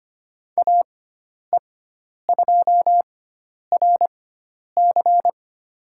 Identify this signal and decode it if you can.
Morse code, heard as a keyed tone, about 25 wpm, spaced as for 10 wpm, 700 Hz. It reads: AE2RC
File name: Lesson 16g - Letter c - Call Signs 25wpm